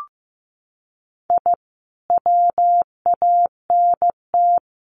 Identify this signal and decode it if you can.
Morse code, heard as a keyed tone, about 15 wpm, 700 Hz. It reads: I WANT
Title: Sets of 5 Words Encore2 15wpm